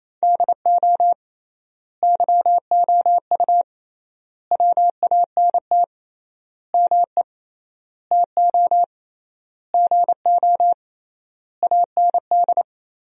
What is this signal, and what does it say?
Morse code, heard as a keyed tone, about 28 wpm, 700 Hz. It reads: DO YOU WANT ME TO GO AND
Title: 3x Word-Spacing - Sentences from Top 200 Words - 28wpm